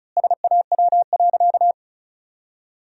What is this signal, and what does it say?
Morse code, heard as a keyed tone, about 36 wpm, 700 Hz. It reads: SAW.
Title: 2x Word-Spacing - Sentences from Top 300 Words - 35wpm